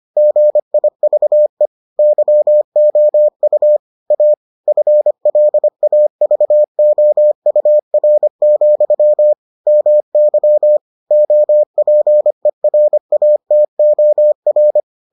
Morse code, heard as a keyed tone, about 25 wpm, 600 Hz. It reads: GIVE YOU A FLAVOUR, MY OPERATOR